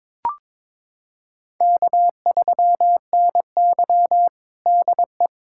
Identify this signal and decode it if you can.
Morse code, heard as a keyed tone, about 22 wpm, 700 Hz. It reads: K3NY DE
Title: USA Callsign DE Callsign - 4 Characters - 22wpm